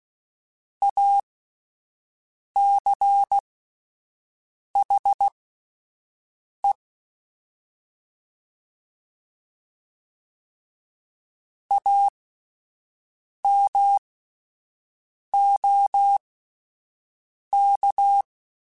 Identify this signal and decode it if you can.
Morse code, heard as a keyed tone, about 16 wpm, spaced as for 5 wpm, 800 Hz. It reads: ACHE AMOK
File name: Alphabetical / Alphabetical mp3s / As @5